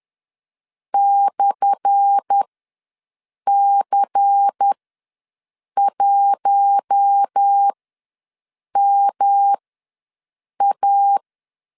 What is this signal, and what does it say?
Morse code, heard as a keyed tone, about 10 wpm, spaced as for 6 wpm, 800 Hz. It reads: /C1MA
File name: LEZIONI / 10 SIGNS / 10A SLASH ws 10wpm 4121